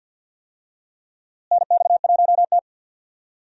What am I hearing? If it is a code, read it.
Morse code, heard as a keyed tone, about 53 wpm, 700 Hz. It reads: NX1T
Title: USA Call Signs - 4 Characters - Repeated 3x - 50wpm